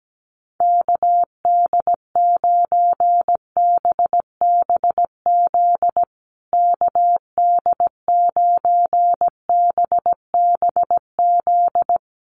Morse code, heard as a keyed tone, about 17 wpm, 700 Hz. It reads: KD9BBZ KD9BBZ
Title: USA Call Signs Repeated 3x 17wpm